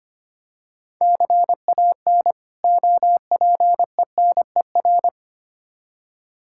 Morse code, heard as a keyed tone, about 25 wpm, 700 Hz. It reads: CAN OPENER
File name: practice / 25wpm / Difficult Words with Hints - 25wpm